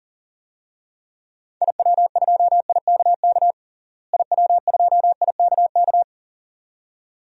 Morse code, heard as a keyed tone, about 40 wpm, 700 Hz. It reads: IW2IKK IW2IKK
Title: DXCC-100 Call Signs - 6 Characters - Repeated 2x - 40wpm